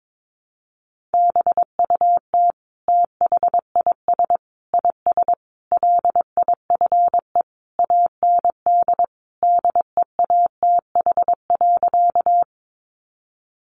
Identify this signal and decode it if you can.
Morse code, heard as a keyed tone, about 22 wpm, 700 Hz. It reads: BUT THIS IS LIFE AND DEATH.
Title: Sentences from Top 1100 Words 22wpm